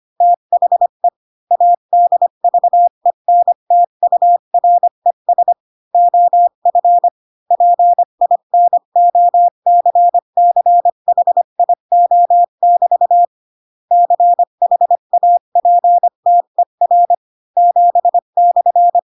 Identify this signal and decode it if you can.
Morse code, heard as a keyed tone, about 25 wpm, 700 Hz. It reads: THE ADVENTURES OF PINOCCHIO<BT> CHAPTER 7/
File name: Pinocchio / Pinocchio ch7 20 25 ik1hge